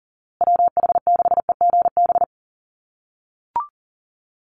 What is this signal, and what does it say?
Morse code, heard as a keyed tone, about 40 wpm, 700 Hz. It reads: WH6EGB
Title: USA Call Signs Repeated 2x 40wpm